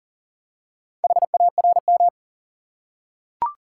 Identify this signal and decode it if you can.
Morse code, heard as a keyed tone, about 40 wpm, 700 Hz. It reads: HARM
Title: Four Letter Words - 40wpm